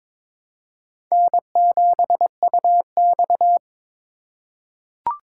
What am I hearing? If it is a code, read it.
Morse code, heard as a keyed tone, about 22 wpm, 700 Hz. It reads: N7UX
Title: USA Call Signs - 4 Characters - 3x Character Spacing - 22wpm